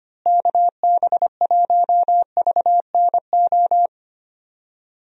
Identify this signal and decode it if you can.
Morse code, heard as a keyed tone, about 25 wpm, 700 Hz. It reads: KB1VNO
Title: USA Call Signs - 6 Characters - Repeated 3x - 25wpm